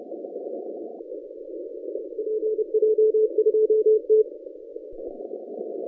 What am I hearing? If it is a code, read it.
Morse code, heard as a keyed tone, about 30 wpm, 440 Hz. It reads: PJ2T
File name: 2015 cq ww cw / 2015 cqwww k5tr - 0-2